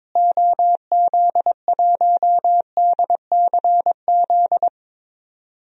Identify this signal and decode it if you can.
Morse code, heard as a keyed tone, about 22 wpm, 700 Hz. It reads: OZ1DCZ
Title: DXCC-100 Call Signs - 6 Characters - Repeated 2x - 22wpm